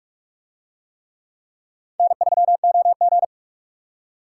Morse code, heard as a keyed tone, about 43 wpm, 700 Hz. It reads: N3OG